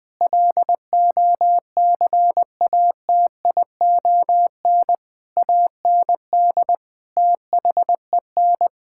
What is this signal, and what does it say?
Morse code, heard as a keyed tone, about 21 wpm, 700 Hz. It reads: LOCATION AND THEN
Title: Sentences from Top 1000 Words - Review Part 6 - 20wpm